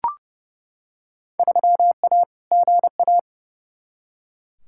Morse code, heard as a keyed tone, about 30 wpm, 700 Hz. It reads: 3A GA
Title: Field Day Exchanges - 30wpm